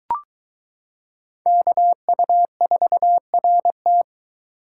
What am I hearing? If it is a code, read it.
Morse code, heard as a keyed tone, about 22 wpm, 700 Hz. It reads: KU4RT